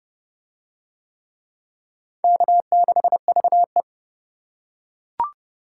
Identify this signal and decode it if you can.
Morse code, heard as a keyed tone, about 27 wpm, 700 Hz. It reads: K6VE